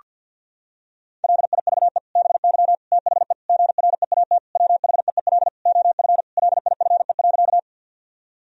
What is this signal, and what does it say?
Morse code, heard as a keyed tone, about 50 wpm, 700 Hz. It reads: LIVE BY THE GREAT WHEEL OF LIFE.
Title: Sentences from Top 1000 Words - Review Part 3 - 50wpm